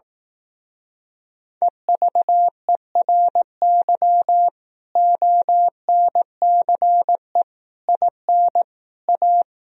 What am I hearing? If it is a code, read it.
Morse code, heard as a keyed tone, about 18 wpm, 700 Hz. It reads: EVERY ONCE IN A
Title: Speed-Racing - Sets of 5 Words 15wpm